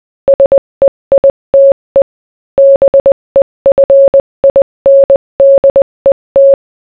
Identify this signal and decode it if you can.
Morse code, heard as a keyed tone, about 20 wpm, 550 Hz. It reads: SEITE BEFINDET